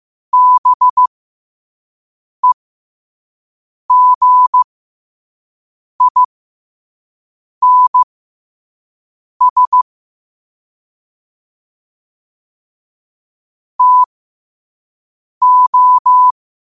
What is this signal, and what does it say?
Morse code, heard as a keyed tone, about 15 wpm, spaced as for 5 wpm, 1000 Hz. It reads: BEGINS TO
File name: qotdfiles / 5wpm